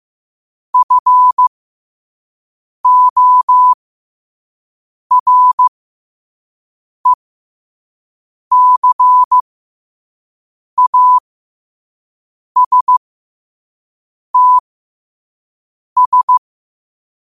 Morse code, heard as a keyed tone, about 15 wpm, spaced as for 5 wpm, 1000 Hz. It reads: FORECASTS
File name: qotdfiles / 5wpm